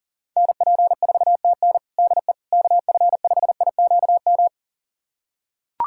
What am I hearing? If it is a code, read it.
Morse code, heard as a keyed tone, about 40 wpm, 700 Hz. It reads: NP4TN DE KF5IQM
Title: USA Callsign DE Callsign 40wpm